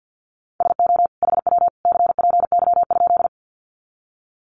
Morse code, heard as a keyed tone, about 50 wpm, 700 Hz. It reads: SO HW CPY?
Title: Common QSO Phrases 50wpm